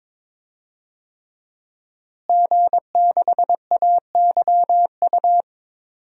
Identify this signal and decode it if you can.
Morse code, heard as a keyed tone, about 22 wpm, 700 Hz. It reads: G6AYU
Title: DXCC-100 Call Signs - 5 Characters - Repeated 3x - 22wpm